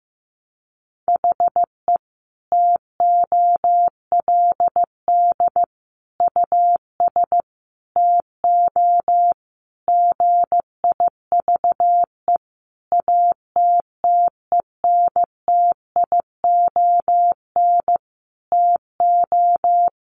Morse code, heard as a keyed tone, about 15 wpm, 700 Hz. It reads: HE TOLD US TO GIVE ATTENTION TO
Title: Sentences from Top 1000 Words 15wpm